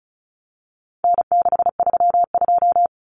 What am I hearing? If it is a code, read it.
Morse code, heard as a keyed tone, about 34 wpm, 700 Hz. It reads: N632
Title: Common Cut Numbers - 4-Digit 35wpm